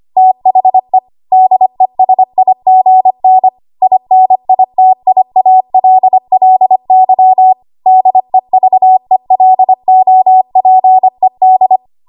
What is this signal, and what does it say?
Morse code, heard as a keyed tone, about 25 wpm, 750 Hz. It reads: THE DESIGN INITIALLY DEVELOPED
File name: Archive / 25 WPM / 25WPM